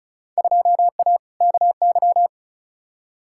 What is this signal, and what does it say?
Morse code, heard as a keyed tone, about 36 wpm, 700 Hz. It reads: 2A KY
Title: Field Day Exchanges - 35wpm